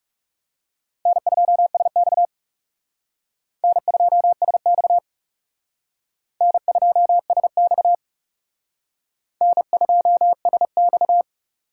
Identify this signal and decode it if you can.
Morse code, heard as a keyed tone, about 37 wpm, 700 Hz. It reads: N2SX N2SX N2SX N2SX
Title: Speed Racing - USA 1x2 Call Signs 30wpm